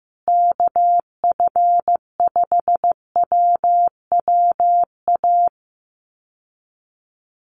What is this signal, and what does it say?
Morse code, heard as a keyed tone, about 15 wpm, 700 Hz. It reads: KF5WWA
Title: USA Call Signs Repeated 3x 15wpm